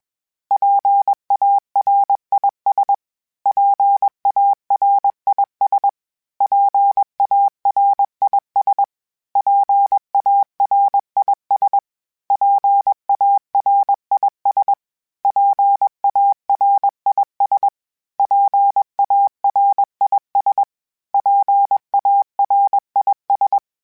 Morse code, heard as a keyed tone, about 21 wpm, 800 Hz. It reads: PARIS PARIS PARIS PARIS PARIS PARIS PARIS PARIS
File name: PARIS Calibration / PARIS mp3s / PARIS@20